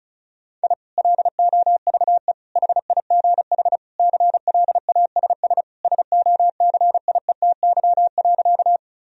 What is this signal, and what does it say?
Morse code, heard as a keyed tone, about 36 wpm, 700 Hz. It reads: I LOVE HIGH CLASS SOCIETY.
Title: Sentences from Top 1000 Words - Review Part 4 - 35wpm